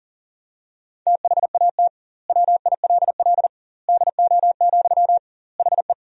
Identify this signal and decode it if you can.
Morse code, heard as a keyed tone, about 40 wpm, 700 Hz. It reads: THAT WILL DO, HE
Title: 2x Word-Spacing - Sentences from Top 100 Words - 40wpm